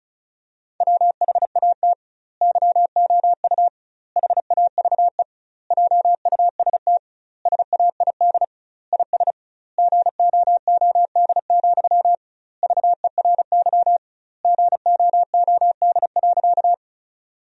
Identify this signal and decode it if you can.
Morse code, heard as a keyed tone, about 34 wpm, 700 Hz. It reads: WHAT YOU HAVE JUST SAID IS GOOD, VERY GOOD.
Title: 2x Word-Spacing - Sentences from Top 200 Words - 35wpm